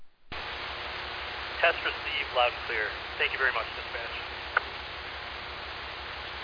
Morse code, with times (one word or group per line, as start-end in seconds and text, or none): none